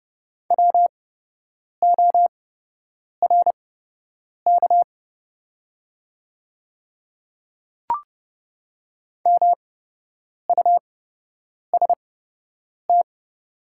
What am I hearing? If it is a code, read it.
Morse code, heard as a keyed tone, about 30 wpm, spaced as for 8 wpm, 700 Hz. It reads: WORK MUST
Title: 8x Character Spacing - Top 200 Words - 30wpm